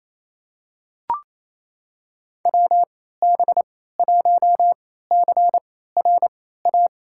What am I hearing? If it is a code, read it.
Morse code, heard as a keyed tone, about 27 wpm, spaced as for 16 wpm, 700 Hz. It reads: WB1CRA
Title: USA Call Signs - 6 Characters - 3x Character Spacing - 28wpm